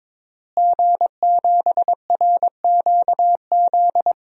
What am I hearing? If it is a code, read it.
Morse code, heard as a keyed tone, about 22 wpm, 700 Hz. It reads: G7RQZ
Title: DXCC-100 Call Signs - 5 Characters - Repeated 2x - 22wpm